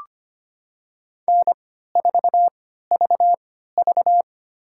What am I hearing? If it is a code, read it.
Morse code, heard as a keyed tone, about 25 wpm, spaced as for 14 wpm, 700 Hz. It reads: N4VV
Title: USA Call Signs - 4 Characters - 3x Character Spacing - 25wpm